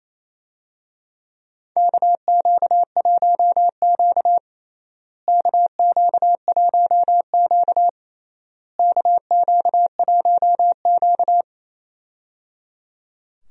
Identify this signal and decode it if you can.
Morse code, heard as a keyed tone, about 27 wpm, 700 Hz. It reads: KQ1Q KQ1Q KQ1Q
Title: USA Call Signs - 4 Characters - Repeated 3x - 28wpm